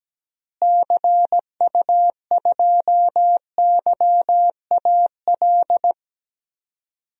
Morse code, heard as a keyed tone, about 17 wpm, 700 Hz. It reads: CU2YAL
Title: DXCC-100 Call Signs - 6 Characters - Repeated 2x - 17wpm